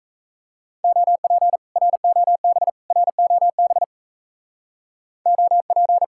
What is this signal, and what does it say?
Morse code, heard as a keyed tone, about 34 wpm, 700 Hz. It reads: OP ROB ROB OP
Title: Speed-Racing Common QSO Phrases - 28wpm